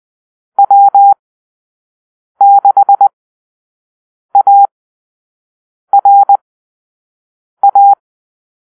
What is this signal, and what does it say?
Morse code, heard as a keyed tone, about 20 wpm, spaced as for 6 wpm, 800 Hz. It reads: W6ARA
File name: arrl / Call-Sign Drill-5